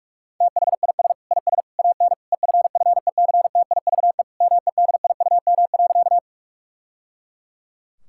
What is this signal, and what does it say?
Morse code, heard as a keyed tone, about 46 wpm, 700 Hz. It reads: THIS IS AN EFFECTIVE MEDIUM.
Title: Sentences from Top 1000 Words - Review Part 6 - 45wpm